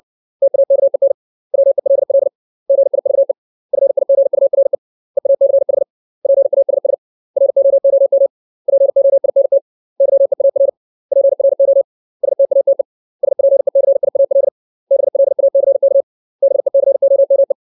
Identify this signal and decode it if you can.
Morse code, heard as a keyed tone, about 60 wpm, 550 Hz. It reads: TAKEN WELL GIVE FIGURE EACH PASS ROOM POINT YEAR PRO STATE SPECIAL BLACK BECOME
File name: 632 words / 632 Words 60WPM